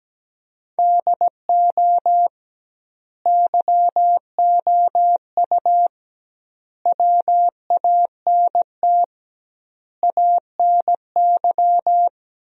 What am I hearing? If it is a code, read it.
Morse code, heard as a keyed tone, about 17 wpm, 700 Hz. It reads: DO YOU WANT ANY
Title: 2x Word-Spacing - Sentences from Top 400 Words - 17wpm